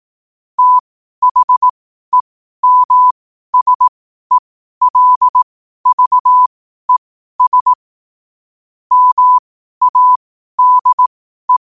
Morse code, heard as a keyed tone, about 18 wpm, spaced as for 12 wpm, 1000 Hz. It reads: THEMSELVES MADE
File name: qotdfiles / 12wpm